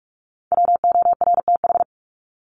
Morse code, heard as a keyed tone, about 44 wpm, 700 Hz. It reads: WORTH
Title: Top 501-750 Words 45wpm